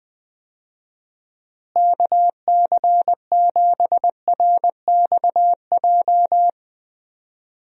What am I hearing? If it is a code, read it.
Morse code, heard as a keyed tone, about 20 wpm, 700 Hz. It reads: KC7RXJ
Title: USA Call Signs - 6 Characters - Repeated 3x - 20wpm